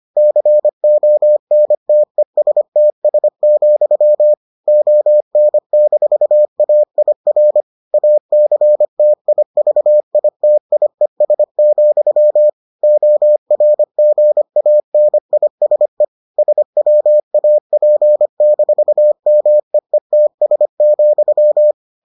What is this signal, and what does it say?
Morse code, heard as a keyed tone, about 25 wpm, 600 Hz. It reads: CONTESTS, ON-AIR ACTIVITIES, ORGANISE SWAP-MEETS,